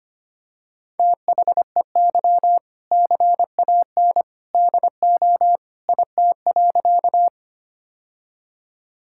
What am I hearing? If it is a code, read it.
Morse code, heard as a keyed tone, about 25 wpm, 700 Hz. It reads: THEY CAN DO IT.
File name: practice / 25wpm / Sentences with 2-Word Phrases - 25wpm